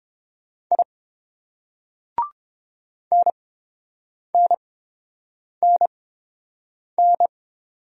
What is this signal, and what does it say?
Morse code, heard as a keyed tone, about 30 wpm, spaced as for 7 wpm, 700 Hz. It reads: I NNNN